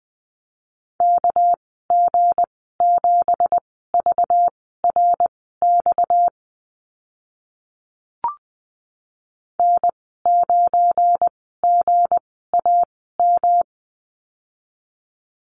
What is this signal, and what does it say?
Morse code, heard as a keyed tone, about 20 wpm, spaced as for 15 wpm, 700 Hz. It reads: KG7VRX N9GAM
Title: 238 Lesson 38g - Letter q - Call Signs 20wpm